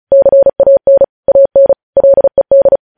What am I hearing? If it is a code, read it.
Morse code, heard as a keyed tone, about 36 wpm, 550 Hz. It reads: CAN AN LED
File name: podcast / News headlines-1x-35x35